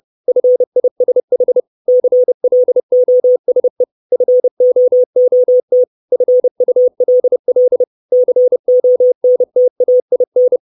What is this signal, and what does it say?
Morse code, heard as a keyed tone, about 30 wpm, 500 Hz. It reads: FISH CLOSE FOOT FULL CONTAIN
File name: 632 Words 30WPM